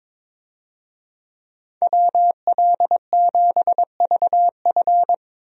Morse code, heard as a keyed tone, about 22 wpm, 700 Hz. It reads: WL7VF
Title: USA Call Signs - 5 Characters - 22wpm